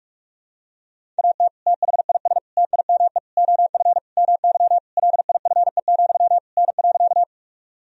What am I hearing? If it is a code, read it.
Morse code, heard as a keyed tone, about 46 wpm, 700 Hz. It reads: AT THIS TIME OF MY LIFE, N.
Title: Sentences from Top 300 Words and Abbreviations 45wpm